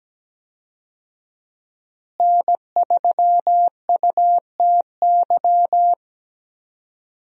Morse code, heard as a keyed tone, about 17 wpm, 700 Hz. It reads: N3UTY
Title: USA Call Signs - 5 Characters - Repeated 2x - 17wpm